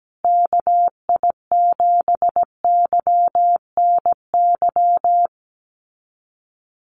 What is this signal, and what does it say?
Morse code, heard as a keyed tone, about 17 wpm, 700 Hz. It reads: KI7YNY
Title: USA Call Signs Repeated 3x 17wpm